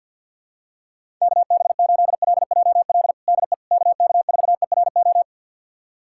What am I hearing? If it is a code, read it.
Morse code, heard as a keyed tone, about 50 wpm, 700 Hz. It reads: KB8LJL DE KK4ERO
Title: USA Callsign DE Callsign 50wpm